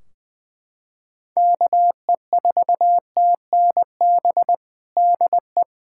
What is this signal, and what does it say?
Morse code, heard as a keyed tone, about 21 wpm, 700 Hz. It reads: KE4TNB DE
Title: Practice QSO - Exchange 4 - Ending - 20wpm